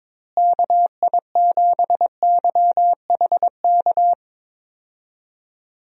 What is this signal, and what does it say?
Morse code, heard as a keyed tone, about 22 wpm, 700 Hz. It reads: KI7YHK
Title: USA Call Signs - 6 Characters - Repeated 3x - 22wpm